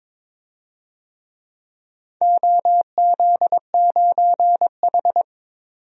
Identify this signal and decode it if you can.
Morse code, heard as a keyed tone, about 21 wpm, 700 Hz. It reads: OZ9H
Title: DXCC-100 Call Signs - 4 Characters - 22wpm